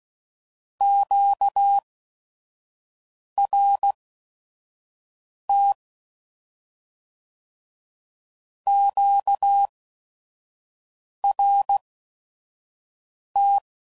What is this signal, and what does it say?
Morse code, heard as a keyed tone, about 16 wpm, spaced as for 5 wpm, 800 Hz. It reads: QRT QRT
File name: CW Code Groups @5WPM / Q Calls / Q Calls Twice